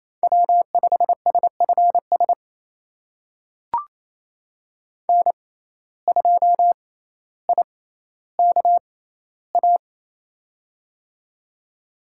Morse code, heard as a keyed tone, about 28 wpm, spaced as for 14 wpm, 700 Hz. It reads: W5SFS N2IKA
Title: USA Call Signs - 5 Characters - 6x Character Spacing - 28wpm